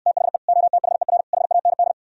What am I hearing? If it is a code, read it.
Morse code, heard as a keyed tone, about 68 wpm, 700 Hz. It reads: THE OTHER HAND